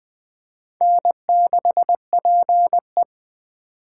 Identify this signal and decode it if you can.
Morse code, heard as a keyed tone, about 21 wpm, 700 Hz. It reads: N6PE
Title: Speed Racing - USA 1x2 Call Signs 17wpm